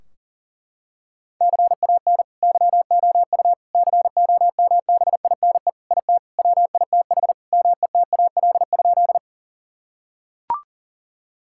Morse code, heard as a keyed tone, about 37 wpm, 700 Hz. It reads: CAN YOU COMBINE IT WITH METAL?
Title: Sentences from Top 1600 Words 40wpm